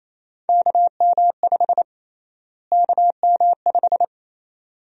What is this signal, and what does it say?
Morse code, heard as a keyed tone, about 27 wpm, 700 Hz. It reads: KM5 KM5
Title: USA Call Sign Prefixes - Repeated 2x - 28wpm